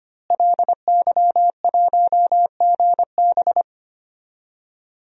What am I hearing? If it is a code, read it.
Morse code, heard as a keyed tone, about 25 wpm, 700 Hz. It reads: LY1GB
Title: DXCC-100 Call Signs - 5 Characters - Repeated 3x - 25wpm